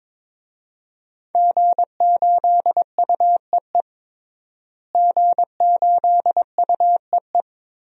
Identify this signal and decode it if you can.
Morse code, heard as a keyed tone, about 23 wpm, 700 Hz. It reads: G8UEE G8UEE
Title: DXCC-100 Call Signs - 5 Characters - Repeated 3x - 22wpm